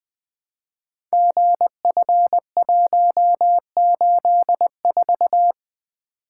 Morse code, heard as a keyed tone, about 20 wpm, 700 Hz. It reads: GF184
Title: Code Groups - 5 Letter:Numbers - 20wpm